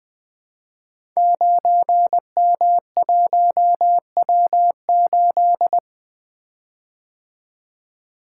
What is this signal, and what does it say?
Morse code, heard as a keyed tone, about 20 wpm, 700 Hz. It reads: 9M1W8
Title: Code Groups - 5 Letter:Numbers - 20wpm